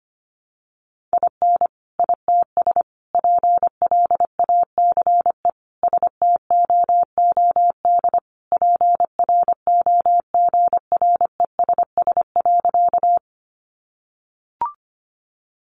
Morse code, heard as a keyed tone, about 25 wpm, 700 Hz. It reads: IN ITS PLACE STOOD PROGRESS.
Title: Sentences from Top 1400 Words 25wpm